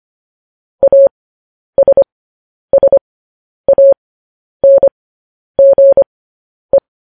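Morse code, heard as a keyed tone, about 25 wpm, spaced as for 10 wpm, 550 Hz. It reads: ASSANGE
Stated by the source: Morse news headlines practice at 10.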